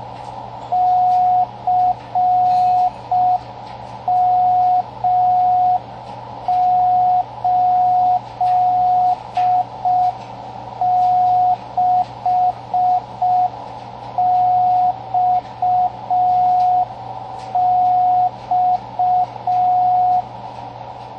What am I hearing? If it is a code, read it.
Morse code, heard as a keyed tone, about 5 wpm, 730 Hz, with radio noise. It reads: CM86XX
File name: World's Slowest Beacon